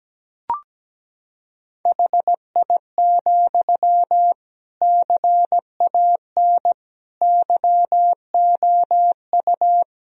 Morse code, heard as a keyed tone, about 17 wpm, 700 Hz. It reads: HI, CAN YOU
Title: Sentences from Top 1000 Words - Review Part 3 - 17wpm